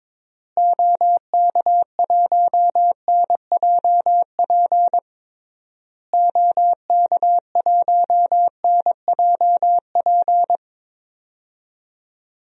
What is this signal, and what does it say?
Morse code, heard as a keyed tone, about 21 wpm, 700 Hz. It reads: OK1NJP OK1NJP
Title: DXCC-100 Call Signs - 6 Characters - Repeated 3x - 22wpm